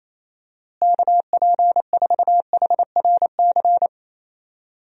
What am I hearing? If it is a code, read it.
Morse code, heard as a keyed tone, about 28 wpm, 700 Hz. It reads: KP4HRC
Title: USA Call Signs - 6 Characters - Repeated 2x - 28wpm